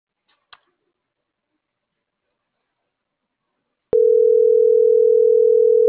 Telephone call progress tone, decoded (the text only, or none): ringback tone